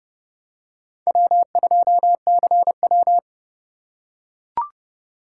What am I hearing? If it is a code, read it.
Morse code, heard as a keyed tone, about 29 wpm, 700 Hz. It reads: W2CW